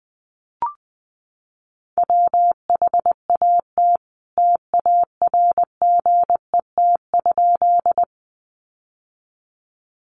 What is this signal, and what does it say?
Morse code, heard as a keyed tone, about 20 wpm, 700 Hz. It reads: WHAT TARGET?
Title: Sentences from Top 1200 Words 20wpm